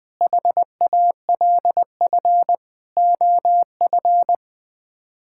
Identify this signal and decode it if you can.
Morse code, heard as a keyed tone, about 20 wpm, 700 Hz. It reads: HALF OF
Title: Build A Phrase - 5-Word Phrases - 20wpm